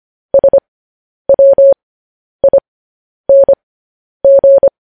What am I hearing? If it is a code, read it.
Morse code, heard as a keyed tone, about 25 wpm, spaced as for 10 wpm, 550 Hz. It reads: SWING
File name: podcast / News headlines-1x-25x10